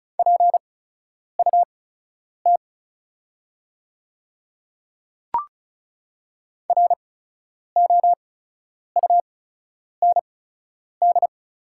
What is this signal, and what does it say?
Morse code, heard as a keyed tone, about 34 wpm, spaced as for 10 wpm, 690 Hz. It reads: PUT ROUND
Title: 8x Character Spacing - Top 200 Words - 35wpm